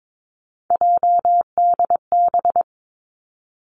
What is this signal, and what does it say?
Morse code, heard as a keyed tone, about 22 wpm, 700 Hz. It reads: JDB
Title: Three Challenging Letter-Number Combinations 22wpm